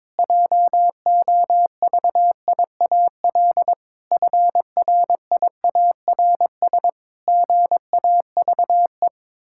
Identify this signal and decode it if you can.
Morse code, heard as a keyed tone, about 22 wpm, 700 Hz. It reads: JOVIAL FRIARS GAVE
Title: Pangram Challenge v2 - 22wpm